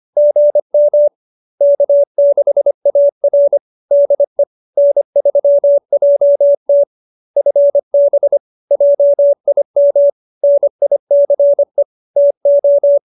Morse code, heard as a keyed tone, about 25 wpm, 570 Hz. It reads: GM K6AR DE N3JT FB JIM NICE TO